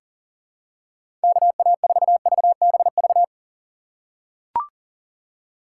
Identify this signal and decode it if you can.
Morse code, heard as a keyed tone, about 40 wpm, 700 Hz. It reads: KA4VBV
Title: USA Call Signs - 2x Character Spacing - 40wpm